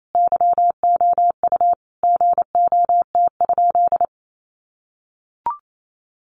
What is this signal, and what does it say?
Morse code, heard as a keyed tone, about 28 wpm, 700 Hz. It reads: YOU GOT?
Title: Sentences from Top 300 Words 28wpm